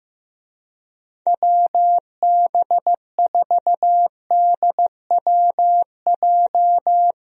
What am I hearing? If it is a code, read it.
Morse code, heard as a keyed tone, about 15 wpm, 700 Hz. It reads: WB4DWJ
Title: USA Call Signs - 6 Characters - Repeated 2x - 15wpm